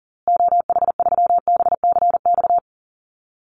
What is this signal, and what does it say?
Morse code, heard as a keyed tone, about 40 wpm, 700 Hz. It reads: OH3BCX
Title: Call Signs Repeated 2x - 40wpm